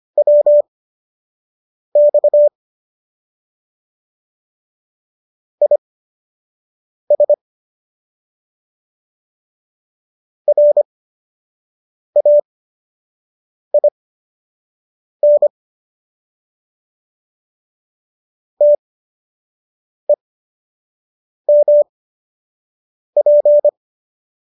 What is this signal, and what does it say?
Morse code, heard as a keyed tone, about 25 wpm, spaced as for 6 wpm, 590 Hz. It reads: WX IS RAIN TEMP
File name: qso002 06